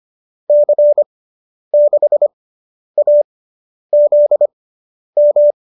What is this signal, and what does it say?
Morse code, heard as a keyed tone, about 25 wpm, spaced as for 10 wpm, 590 Hz. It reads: C6AZM